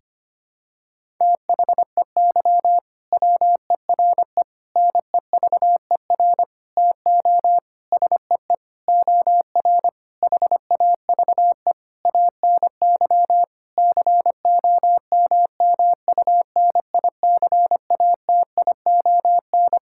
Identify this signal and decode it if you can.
Morse code, heard as a keyed tone, about 25 wpm, 700 Hz. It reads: THEY WERE NEVER TO SEE OR HAVE ANY COMMUNICATION
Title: Sentences from Top 1000 Words - Review Part 6 - 25wpm